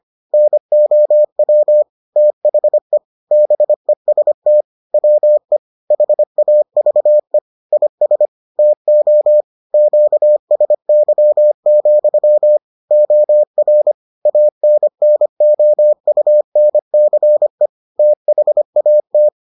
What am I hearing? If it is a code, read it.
Morse code, heard as a keyed tone, about 25 wpm, 600 Hz. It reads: NOW THE BEST WE HAVE IS TO QSY, OR ANNOUNCE THAT